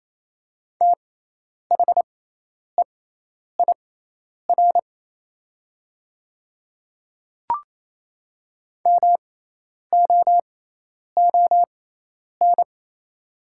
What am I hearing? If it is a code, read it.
Morse code, heard as a keyed tone, about 28 wpm, spaced as for 10 wpm, 700 Hz. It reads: THEIR MOON
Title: 6x Character Spacing - Top 500 Words - 28wpm